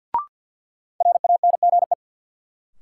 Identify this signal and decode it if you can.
Morse code, heard as a keyed tone, about 53 wpm, 700 Hz. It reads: RANGE
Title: Sets of 4 Words - New Word Drill - 50wpm